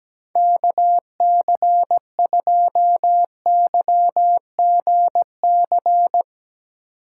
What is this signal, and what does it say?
Morse code, heard as a keyed tone, about 17 wpm, 700 Hz. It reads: KC2YGC
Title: USA Call Signs - 6 Characters - 3x Character Spacing - 17wpm